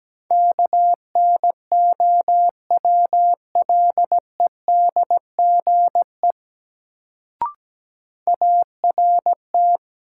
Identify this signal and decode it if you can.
Morse code, heard as a keyed tone, about 17 wpm, 700 Hz. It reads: KNOWLEDGE ART
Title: Sentences from Top 600 Words - New Word Drill - 17wpm